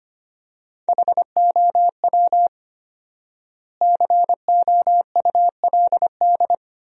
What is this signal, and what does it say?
Morse code, heard as a keyed tone, about 25 wpm, 700 Hz. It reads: HOW COULD
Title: 4x Word-Spacing - Sentences from Top 400 Words - 25wpm